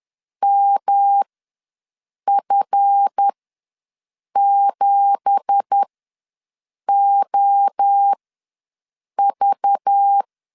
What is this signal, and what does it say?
Morse code, heard as a keyed tone, about 10 wpm, spaced as for 6 wpm, 800 Hz. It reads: MF7OV
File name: LEZIONI / 9 2867 ws 10wpm 4215